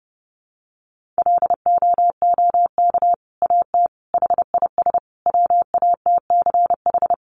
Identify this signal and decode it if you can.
Morse code, heard as a keyed tone, about 30 wpm, 700 Hz. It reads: LOOK AT HIS WATCH
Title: Sentences from Top 1500 Words 30wpm